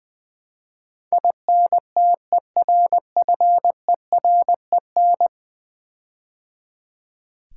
Morse code, heard as a keyed tone, about 20 wpm, 700 Hz. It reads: INTERFEREN
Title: Build A Word - 5 Letters At A Time - 12-Letter Words - 20wpm